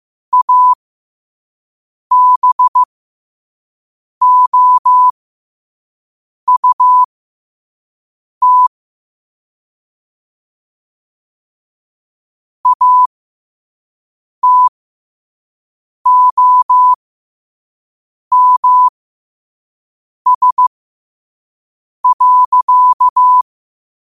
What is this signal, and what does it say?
Morse code, heard as a keyed tone, about 15 wpm, spaced as for 5 wpm, 1000 Hz. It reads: ABOUT ATOMS.
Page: New quotes every day in morse code at 5 Words per minute.